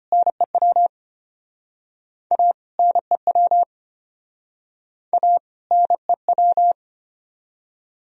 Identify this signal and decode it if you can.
Morse code, heard as a keyed tone, about 33 wpm, 700 Hz. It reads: NEW A NEW A NEW